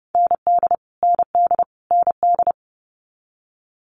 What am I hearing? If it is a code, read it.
Morse code, heard as a keyed tone, about 30 wpm, 700 Hz. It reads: ND ND ND
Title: Multi-letter Phonemes Repeated 3x 30wpm